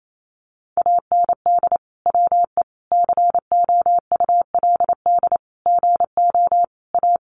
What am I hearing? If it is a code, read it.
Morse code, heard as a keyed tone, about 28 wpm, 700 Hz. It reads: AND WE COULD GO A
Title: Sentences from Top 1200 Words 28wpm